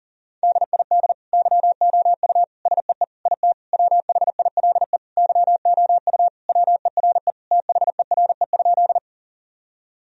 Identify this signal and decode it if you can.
Morse code, heard as a keyed tone, about 40 wpm, 700 Hz. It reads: DID YOU SEE IT WHILE YOU WERE THERE?
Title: Sentences from Top 1000 Words - Review Part 1 - 40wpm